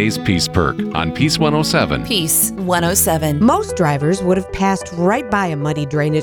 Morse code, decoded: none